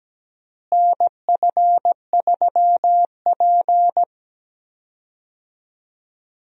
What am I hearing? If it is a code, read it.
Morse code, heard as a keyed tone, about 17 wpm, 700 Hz. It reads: NF3P
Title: USA Call Signs - 4 Characters - Repeated 2x - 17wpm